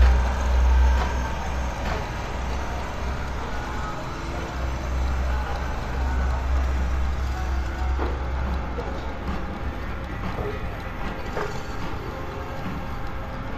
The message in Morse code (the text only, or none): none